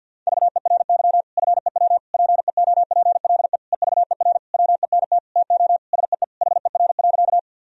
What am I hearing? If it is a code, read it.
Morse code, heard as a keyed tone, about 52 wpm, 700 Hz. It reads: VERY FEW PEOPLE EVER WENT TO SEE HER.
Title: Sentences from Top 1000 Words - Review Part 1 - 50wpm